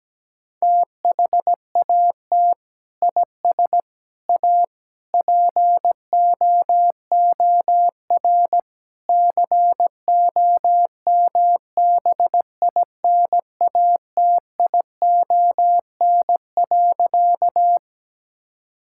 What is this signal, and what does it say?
Morse code, heard as a keyed tone, about 17 wpm, 700 Hz. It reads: THAT IS A POOR COMBINATION.
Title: Sentences from Top 1000 Words - Review Part 6 - 17wpm